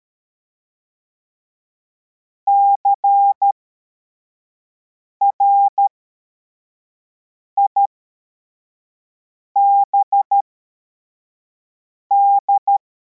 Morse code, heard as a keyed tone, about 13 wpm, spaced as for 4 wpm, 800 Hz. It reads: CRIBD
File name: formation / cw / Text41